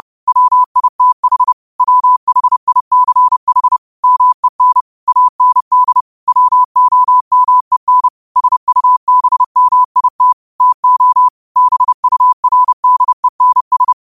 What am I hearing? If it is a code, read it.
Morse code, heard as a keyed tone, about 30 wpm, 1000 Hz. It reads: WITH WHICH MEN AND WOMEN SUBMIT TO BURDENS